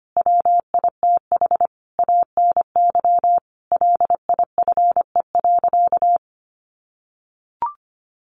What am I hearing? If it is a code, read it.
Morse code, heard as a keyed tone, about 25 wpm, 700 Hz. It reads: WITH ANY LIFE.
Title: Sentences from Top 300 Words 25wpm